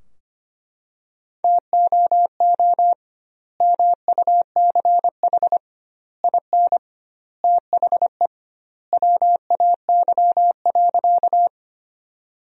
Morse code, heard as a keyed tone, about 25 wpm, 700 Hz. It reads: TOO MUCH IN THE WAY.
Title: 2x Word-Spacing - Sentences from Top 200 Words - 25wpm